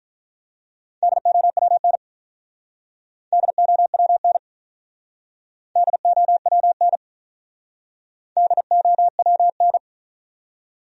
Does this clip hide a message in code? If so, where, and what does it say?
Yes: Morse code from 1.0 to 9.8 s, heard as a keyed tone, about 45 wpm, 700 Hz. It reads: DOWN DOWN DOWN DOWN